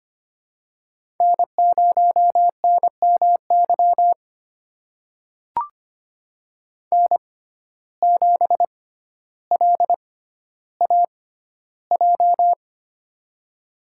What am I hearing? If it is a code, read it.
Morse code, heard as a keyed tone, about 25 wpm, spaced as for 13 wpm, 700 Hz. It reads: N0NMY N7LAJ
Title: USA Call Signs - 5 Characters - 6x Character Spacing - 25wpm